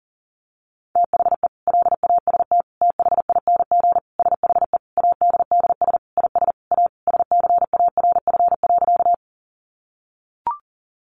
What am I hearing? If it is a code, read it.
Morse code, heard as a keyed tone, about 40 wpm, 700 Hz. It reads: THE LAST THING SHE ADDS IS A SCARF.
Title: Easy Phrases 40wpm